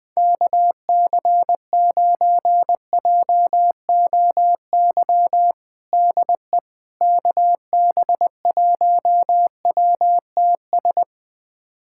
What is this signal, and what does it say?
Morse code, heard as a keyed tone, about 20 wpm, 700 Hz. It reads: KC9JOY DE KB1WTS
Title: USA Callsign DE Callsign - 6 Characters - 20wpm